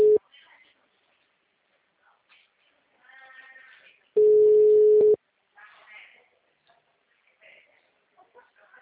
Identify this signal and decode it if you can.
Ringback tone (Europe, 425 Hz), with speech.